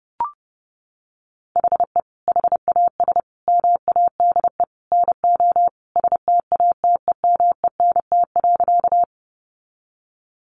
Morse code, heard as a keyed tone, about 29 wpm, 700 Hz. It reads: HE HAS MADE NO STATEMENT.
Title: Sentences from Top 1200 Words 30wpm